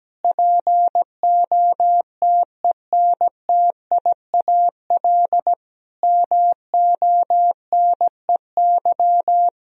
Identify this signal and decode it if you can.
Morse code, heard as a keyed tone, about 17 wpm, 700 Hz. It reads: POTENTIAL MONEY
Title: Sentences from Top 1000 Words - Review Part 4 - 17wpm